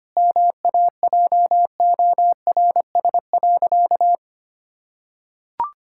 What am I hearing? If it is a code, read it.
Morse code, heard as a keyed tone, about 25 wpm, 700 Hz. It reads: MAJORS.
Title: Pangram Challenge v2 - 25wpm